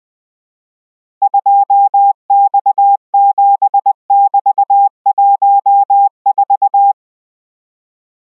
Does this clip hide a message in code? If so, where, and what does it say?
Yes: Morse code from 1.2 to 6.9 s, heard as a keyed tone, about 20 wpm, 800 Hz. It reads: 2X7<BT>14